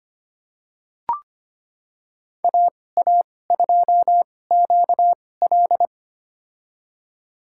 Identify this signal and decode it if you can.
Morse code, heard as a keyed tone, about 25 wpm, spaced as for 18 wpm, 700 Hz. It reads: AA2QL
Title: USA Call Signs - 5 Characters - 2x Character Spacing - 25wpm